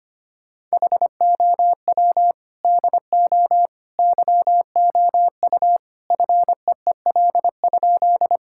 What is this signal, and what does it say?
Morse code, heard as a keyed tone, about 25 wpm, 700 Hz. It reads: HOW DO YOU FEEL?
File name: practice / 25wpm / Sentences with 3-Word Phrases - 25wpm